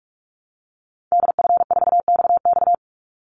Morse code, heard as a keyed tone, about 44 wpm, 700 Hz. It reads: DF4XX
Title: Call Signs Repeated 3x - 45wpm